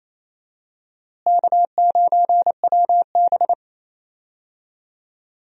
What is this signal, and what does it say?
Morse code, heard as a keyed tone, about 28 wpm, 700 Hz. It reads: K9WB
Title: USA Call Signs - 4 Characters - Repeated 2x - 28wpm